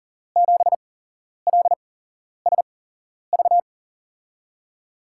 Morse code, heard as a keyed tone, about 40 wpm, spaced as for 11 wpm, 700 Hz. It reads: 7LSV
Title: USA Call Signs - 6 Characters - 8x Character Spacing - 40wpm